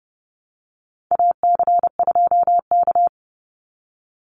Morse code, heard as a keyed tone, about 30 wpm, 700 Hz. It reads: AC2K
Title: Call Signs Repeated 3x - 30wpm